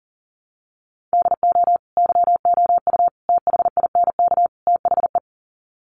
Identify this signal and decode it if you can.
Morse code, heard as a keyed tone, about 40 wpm, 700 Hz. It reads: DO YOU THINK THE
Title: Sets of 4 Words 40wpm